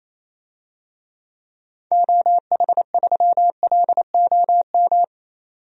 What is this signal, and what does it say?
Morse code, heard as a keyed tone, about 28 wpm, 700 Hz. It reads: OH3LOM
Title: DXCC-100 Call Signs - 6 Characters - 28wpm